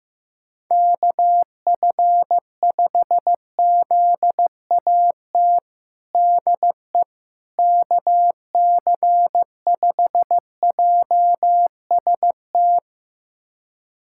Morse code, heard as a keyed tone, about 15 wpm, 700 Hz. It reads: KF5ZAT DE KC5JST
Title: USA Callsign DE Callsign - 6 Characters - 15wpm